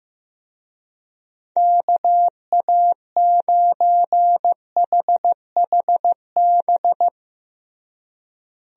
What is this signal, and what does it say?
Morse code, heard as a keyed tone, about 15 wpm, 700 Hz. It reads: KA9HHB
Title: USA Call Signs - 6 Characters - Repeated 3x - 15wpm